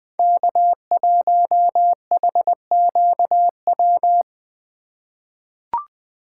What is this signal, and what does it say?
Morse code, heard as a keyed tone, about 20 wpm, 700 Hz. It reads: K1HQW
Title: USA Call Signs - 4x Character Spacing - 20wpm